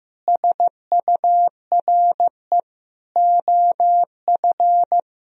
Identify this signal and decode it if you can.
Morse code, heard as a keyed tone, about 15 wpm, 700 Hz. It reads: SURE OF
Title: Sentences from Top 300 Words and Abbreviations 15wpm